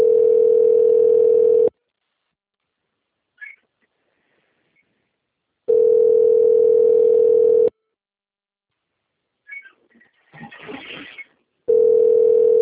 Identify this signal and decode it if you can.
Ringback tone (North America), 440 + 480 Hz.